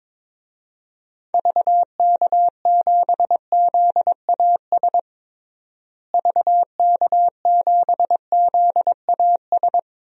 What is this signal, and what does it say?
Morse code, heard as a keyed tone, about 22 wpm, 700 Hz. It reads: VK7ZAS VK7ZAS
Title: DXCC-100 Call Signs - 6 Characters - Repeated 2x - 22wpm